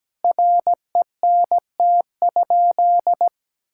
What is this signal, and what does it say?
Morse code, heard as a keyed tone, about 17 wpm, 700 Hz. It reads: RENT?
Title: Sentences from Top 1000 Words - Review Part 5 - 17wpm